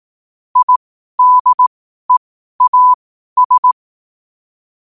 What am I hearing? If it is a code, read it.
Morse code, heard as a keyed tone, about 18 wpm, spaced as for 12 wpm, 1000 Hz. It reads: IDEAS
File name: qotdfiles / 12wpm